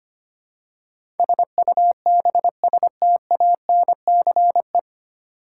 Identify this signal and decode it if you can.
Morse code, heard as a keyed tone, about 25 wpm, 700 Hz. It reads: SUBSTANCE
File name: practice / 25wpm / 9 Letter Words - First 3 Letters Given - 25wpm